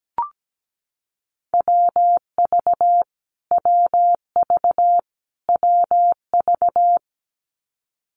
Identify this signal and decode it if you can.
Morse code, heard as a keyed tone, about 17 wpm, 700 Hz. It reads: WV WV WV
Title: US States Repeated 3x 17wpm